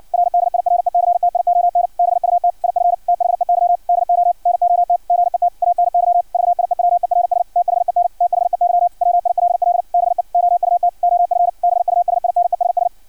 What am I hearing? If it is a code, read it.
Morse code, heard as a keyed tone, about 56 wpm, 700 Hz. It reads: R DIRECTION BUT IF THEY DO NOT GET ANY VIEWERS THEN THEY WILL BE OUT OF BUSINESS
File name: archived / CRSnet